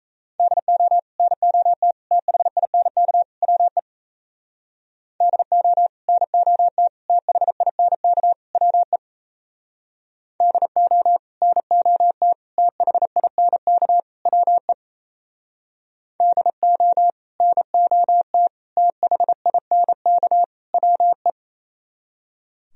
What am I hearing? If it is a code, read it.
Morse code, heard as a keyed tone, about 35 wpm, 700 Hz. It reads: DO NOT THINK WE DO NOT THINK WE DO NOT THINK WE DO NOT THINK WE
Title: Speed-Racing - Sets of 4 Words 28wpm